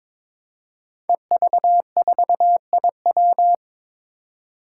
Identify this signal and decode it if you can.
Morse code, heard as a keyed tone, about 22 wpm, 700 Hz. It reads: EV4IW
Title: Five Challenging Letter-Number Combinations 22wpm